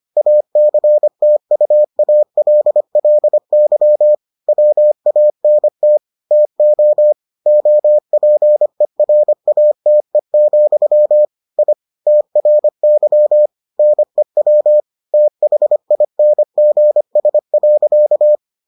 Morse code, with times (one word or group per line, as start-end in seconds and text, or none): 0.2-4.1 ACTUALLY
4.5-6.0 WANT
6.3-7.1 TO
7.5-11.3 OPERATE,
11.6-11.7 I
12.1-13.5 TRY
13.8-14.8 NEW
15.1-18.4 THINGS.